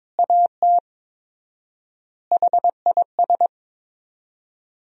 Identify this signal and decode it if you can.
Morse code, heard as a keyed tone, about 22 wpm, 700 Hz. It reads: AT HIS
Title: 4x Word-Spacing - Sentences from Top 300 Words - 22wpm